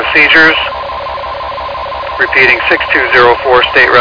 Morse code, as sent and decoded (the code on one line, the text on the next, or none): none